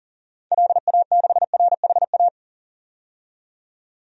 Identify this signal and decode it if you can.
Morse code, heard as a keyed tone, about 40 wpm, 700 Hz. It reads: LA6RHA
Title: DXCC-100 Call Signs - 6 Characters - 40wpm